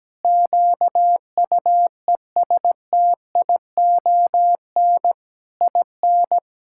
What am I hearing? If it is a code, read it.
Morse code, heard as a keyed tone, about 17 wpm, 700 Hz. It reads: QUESTION IN
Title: Sentences from Top 1000 Words - Review Part 6 - 17wpm